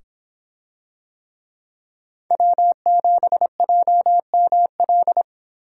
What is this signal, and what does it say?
Morse code, heard as a keyed tone, about 26 wpm, 700 Hz. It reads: W7JML